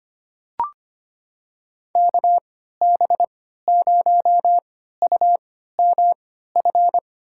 Morse code, heard as a keyed tone, about 25 wpm, spaced as for 14 wpm, 700 Hz. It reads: KB0UMF
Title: USA Call Signs - 6 Characters - 3x Character Spacing - 25wpm